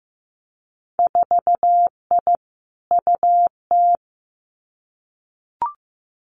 Morse code, heard as a keyed tone, about 15 wpm, 700 Hz. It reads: 4I UT
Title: Winter Field Day Exchanges 15wpm